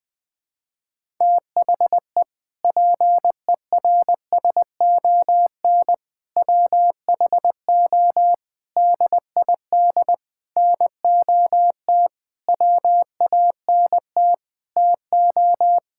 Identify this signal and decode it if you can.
Morse code, heard as a keyed tone, about 21 wpm, 700 Hz. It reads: THE PERSON WHO DID NOT WANT TO
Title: Sentences with 7-Word Phrases - 20wpm